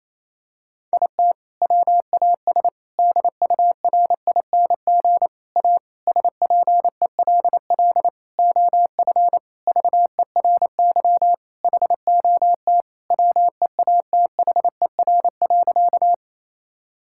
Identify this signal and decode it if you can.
Morse code, heard as a keyed tone, about 28 wpm, 700 Hz. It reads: IT WAS DURING A SPELL OF VERY HOT WEATHER.
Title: Sentences from Top 1000 Words - Review Part 5 - 28wpm